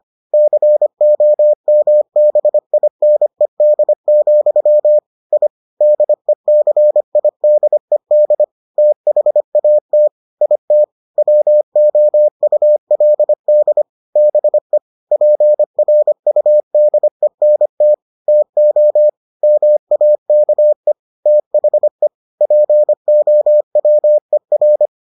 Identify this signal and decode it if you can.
Morse code, heard as a keyed tone, about 25 wpm, 600 Hz. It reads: COMBINED, I DECIDED THAT IT WOULD BE PRUDENT TO MAKE THE POWER